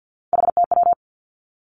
Morse code, heard as a keyed tone, about 50 wpm, 700 Hz. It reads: HTW
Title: Three Challenging Letter-Number Combinations 50wpm